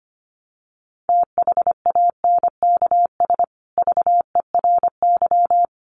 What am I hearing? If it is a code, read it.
Morse code, heard as a keyed tone, about 25 wpm, 700 Hz. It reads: THANKS VERY
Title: Sets of 5 Words Encore2 25wpm